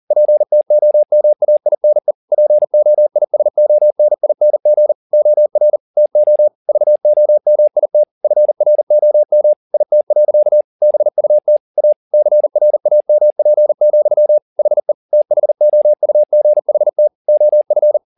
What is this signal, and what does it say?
Morse code, heard as a keyed tone, about 40 wpm, 600 Hz. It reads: PTOMAINE POISONING OR TO VOMIT FROM IT. BUT A CRAMP, HE THOUGHT OF